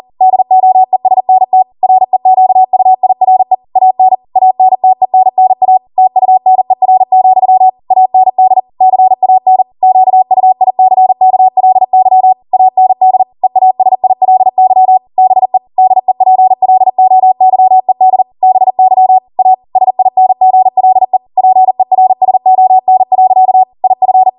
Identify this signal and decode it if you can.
Morse code, heard as a keyed tone, about 40 wpm, 750 Hz. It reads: DOESNT REQUIRE AN ANTENNA TUNER, AND CAN QUICKLY AND EASILY BE DEPLOYED BY A SINGLE PERSON. IF
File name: Morse / 40WPM